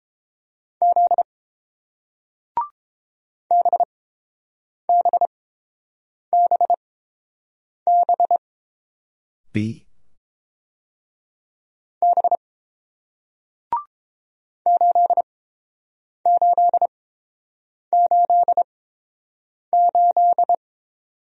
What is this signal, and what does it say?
Morse code, heard as a keyed tone, about 30 wpm, spaced as for 6 wpm, 700 Hz. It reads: Z BBBBB 8888